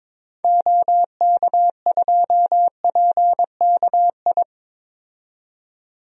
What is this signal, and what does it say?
Morse code, heard as a keyed tone, about 21 wpm, 700 Hz. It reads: OK2PKI
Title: DXCC-100 Call Signs - 6 Characters - 22wpm